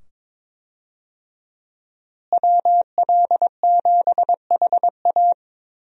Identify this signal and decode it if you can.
Morse code, heard as a keyed tone, about 22 wpm, 700 Hz. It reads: WL7HA
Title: USA Call Signs - 5 Characters - Repeated 2x - 22wpm